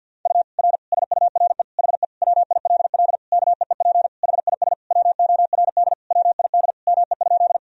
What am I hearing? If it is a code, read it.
Morse code, heard as a keyed tone, about 52 wpm, 700 Hz. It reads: U R SURE HE WILL KEEP HIS WORD WID ME?
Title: Sentences from Top 300 Words and Abbreviations 50wpm